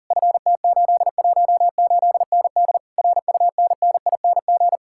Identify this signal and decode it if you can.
Morse code, heard as a keyed tone, about 50 wpm, 700 Hz. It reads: FT818ND RUNNING
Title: Speed-Racing Common QSO Phrases - 40wpm